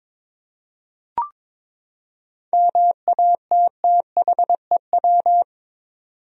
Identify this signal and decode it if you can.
Morse code, heard as a keyed tone, about 22 wpm, 700 Hz. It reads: MATTHEW
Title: USA Common Names - 22wpm